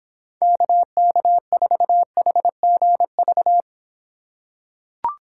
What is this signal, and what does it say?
Morse code, heard as a keyed tone, about 26 wpm, 700 Hz. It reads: KK4HGV